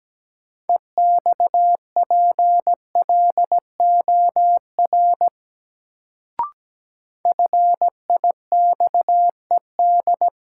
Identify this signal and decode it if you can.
Morse code, heard as a keyed tone, about 17 wpm, 700 Hz. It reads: EXPLOR FIXED
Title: Sentences from Top 1100 Words - New Word Drill - 17wpm